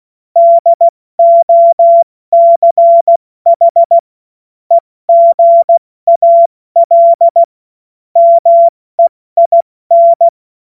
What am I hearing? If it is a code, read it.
Morse code, heard as a keyed tone, about 16 wpm, 680 Hz. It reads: DOCH EGAL MEIN
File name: Morsekurs / Witze / Witz 22. Februar 2022 70 F80